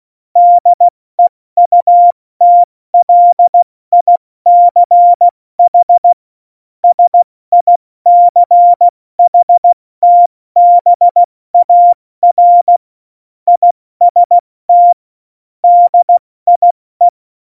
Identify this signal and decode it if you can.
Morse code, heard as a keyed tone, about 16 wpm, 700 Hz. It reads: DEUTLICH SICHTBAR IST DIE
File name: Morsekurs / Witze / Witz 04 August 2020, 70 F80